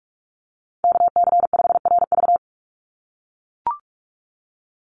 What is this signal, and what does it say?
Morse code, heard as a keyed tone, about 44 wpm, 700 Hz. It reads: KC5RV
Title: USA Call Signs 45wpm